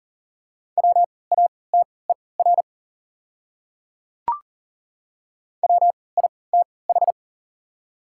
Morse code, heard as a keyed tone, about 40 wpm, spaced as for 23 wpm, 700 Hz. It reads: WATER WITH
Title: 3x Character Spacing - Top 200 Words - 40wpm